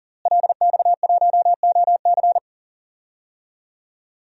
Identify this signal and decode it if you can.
Morse code, heard as a keyed tone, about 40 wpm, 700 Hz. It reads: LX1OC
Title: DXCC-100 Call Signs - 5 Characters - Repeated 2x - 40wpm